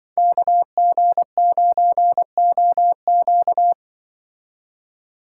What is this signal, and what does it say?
Morse code, heard as a keyed tone, about 24 wpm, 700 Hz. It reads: KG9OQ
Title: Speed Racing - USA 2x2 Call Signs 20wpm